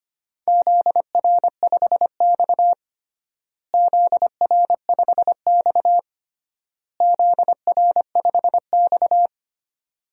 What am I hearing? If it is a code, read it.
Morse code, heard as a keyed tone, about 25 wpm, 700 Hz. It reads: ZR5X ZR5X ZR5X
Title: DXCC-100 Call Signs - 4 Characters - Repeated 3x - 25wpm